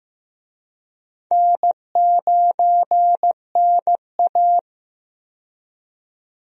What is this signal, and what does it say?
Morse code, heard as a keyed tone, about 15 wpm, 700 Hz. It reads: N9NA
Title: USA Call Signs - 4 Characters - 15wpm